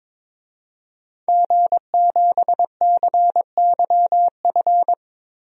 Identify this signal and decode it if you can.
Morse code, heard as a keyed tone, about 21 wpm, 700 Hz. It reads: G7CYF
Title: DXCC-100 Call Signs - 5 Characters - Repeated 2x - 22wpm